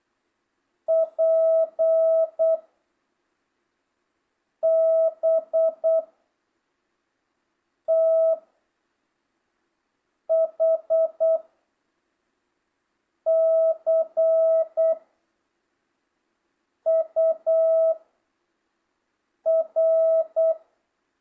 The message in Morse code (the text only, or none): P B T H C U R